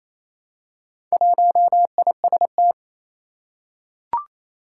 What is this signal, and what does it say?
Morse code, heard as a keyed tone, about 28 wpm, 700 Hz. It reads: 1IST
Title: Four Challenging Letter-Number Combinations 28wpm